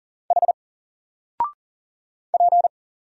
Morse code, heard as a keyed tone, about 40 wpm, 700 Hz. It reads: H P
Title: Single Letter-Number Alt 40wpm